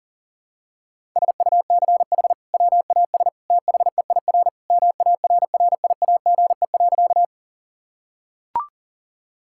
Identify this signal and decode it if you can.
Morse code, heard as a keyed tone, about 37 wpm, 700 Hz. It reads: SUCH WAS THEIR MARRIAGE.
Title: Sentences from Top 1000 Words - Review Part 5 - 40wpm